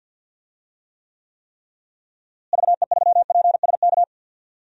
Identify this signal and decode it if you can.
Morse code, heard as a keyed tone, about 52 wpm, 700 Hz. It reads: VE3PSK
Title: DXCC-100 Call Signs - 6 Characters - Repeated 3x - 50wpm